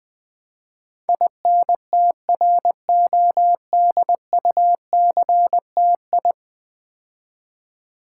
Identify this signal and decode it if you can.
Morse code, heard as a keyed tone, about 20 wpm, 700 Hz. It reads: INTRODUCTI
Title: Build A Word - 5 Letters At A Time - 12-Letter Words - 20wpm